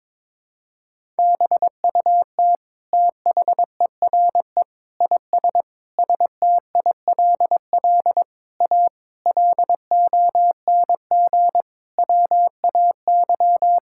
Morse code, heard as a keyed tone, about 22 wpm, 700 Hz. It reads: BUT THERE IS STILL A LONG WAY
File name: Sets of 9 Words - 22wpm